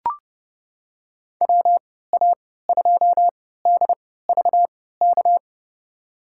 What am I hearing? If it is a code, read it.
Morse code, heard as a keyed tone, about 32 wpm, spaced as for 17 wpm, 700 Hz. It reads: WA2DVK
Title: USA Call Signs - 6 Characters - 3x Character Spacing - 30wpm